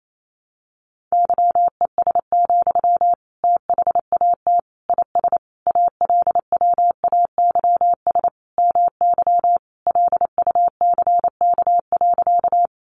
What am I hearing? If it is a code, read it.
Morse code, heard as a keyed tone, about 28 wpm, 700 Hz. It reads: YES, THAT IS ALWAYS MY LUCK.
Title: Sentences from Top 1500 Words 28wpm